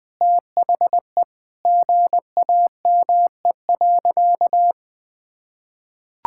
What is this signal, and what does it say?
Morse code, heard as a keyed tone, about 20 wpm, 700 Hz. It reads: THE GAME.
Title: Sentences with 6-Word Phrases - 20wpm